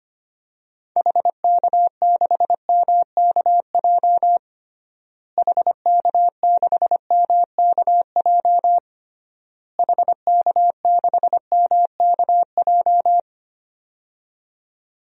Morse code, heard as a keyed tone, about 25 wpm, 700 Hz. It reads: HK6MKJ HK6MKJ HK6MKJ
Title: DXCC-100 Call Signs - 6 Characters - Repeated 3x - 25wpm